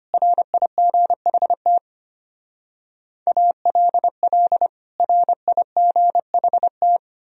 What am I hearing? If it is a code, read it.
Morse code, heard as a keyed tone, about 32 wpm, 700 Hz. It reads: RIGHT ALL RIGHT